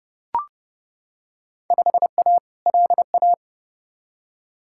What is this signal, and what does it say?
Morse code, heard as a keyed tone, about 30 wpm, 700 Hz. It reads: 5A LA
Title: Field Day Exchanges Encore - 30wpm